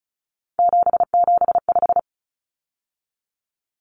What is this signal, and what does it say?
Morse code, heard as a keyed tone, about 34 wpm, 700 Hz. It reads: 775
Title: Numbers - 3-Digit 35wpm